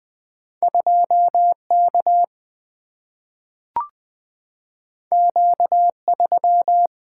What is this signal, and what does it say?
Morse code, heard as a keyed tone, about 20 wpm, 700 Hz. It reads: 2 K Q 3
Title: Single-Letter Single-Number 20wpm